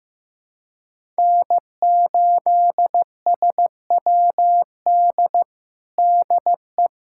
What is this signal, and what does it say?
Morse code, heard as a keyed tone, about 15 wpm, 700 Hz. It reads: N8SWD DE
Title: USA Callsign DE Callsign 15wpm